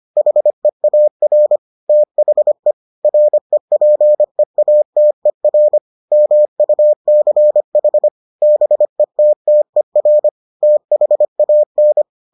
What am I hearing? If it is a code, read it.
Morse code, heard as a keyed tone, about 25 wpm, 600 Hz. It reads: HEAR THE REPEATER MUCH BETTER THAN